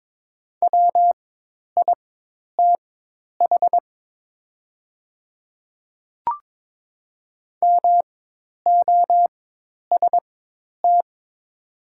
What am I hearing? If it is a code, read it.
Morse code, heard as a keyed tone, about 22 wpm, spaced as for 10 wpm, 700 Hz. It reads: WITH MOST
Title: 4x Character Spacing - Top 100 Words - 22wpm